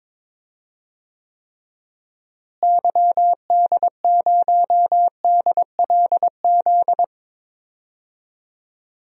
Morse code, heard as a keyed tone, about 22 wpm, 700 Hz. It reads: YD0DLZ
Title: DXCC-100 Call Signs - 6 Characters - Repeated 3x - 22wpm